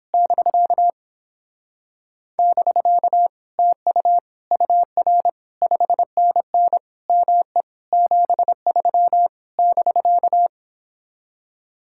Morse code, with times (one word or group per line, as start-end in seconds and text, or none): 0.1-0.9 <BK>
2.4-3.3 <BK>
3.6-4.2 TU
4.5-5.3 UR
5.6-6.8 5NN
7.1-7.6 ME
7.9-9.3 73
9.6-10.5 <BK>